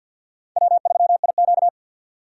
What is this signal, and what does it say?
Morse code, heard as a keyed tone, about 47 wpm, 700 Hz. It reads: W3IQ
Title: USA Call Signs - 4 Characters - 50wpm